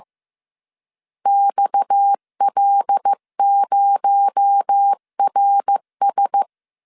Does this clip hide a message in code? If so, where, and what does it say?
Yes: Morse code from 1.3 to 6.4 s, heard as a keyed tone, about 15 wpm, 800 Hz. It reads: XL0RS